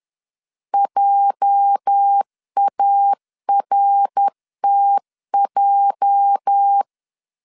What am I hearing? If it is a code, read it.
Morse code, heard as a keyed tone, about 10 wpm, 800 Hz. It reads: JARTJ
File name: LEZIONI / 8 YJ45 / 8 YJ 10wpm 2651